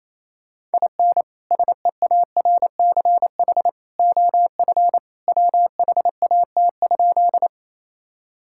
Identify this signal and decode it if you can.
Morse code, heard as a keyed tone, about 28 wpm, 700 Hz. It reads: IN SEARCH OF WHAT?
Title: Sentences from Top 1000 Words - Review Part 5 - 28wpm